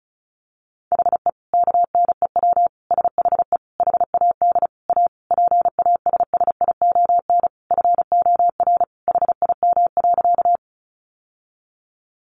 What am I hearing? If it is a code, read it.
Morse code, heard as a keyed tone, about 34 wpm, 700 Hz. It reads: HE KNEW SHE HAD A PASSION FOR HIM.
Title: Sentences from Top 1300 Words 35wpm